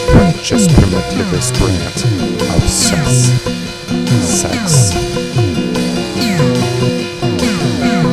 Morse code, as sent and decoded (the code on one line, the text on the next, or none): none